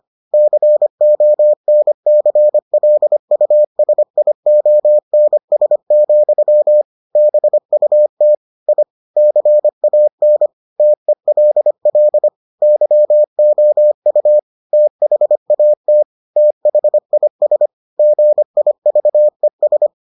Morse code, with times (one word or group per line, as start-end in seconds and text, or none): 0.3-6.8 CONCLUSIONS,
7.2-8.3 BUT
8.7-8.8 I
9.2-10.5 CAN
10.8-12.3 TELL
12.6-14.4 YOU
14.7-16.0 THAT
16.4-17.7 THIS
18.0-19.9 GIVES